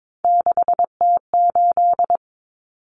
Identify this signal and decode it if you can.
Morse code, heard as a keyed tone, about 22 wpm, 700 Hz. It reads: 6T8
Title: Common Cut Numbers - 3-Digit 22wpm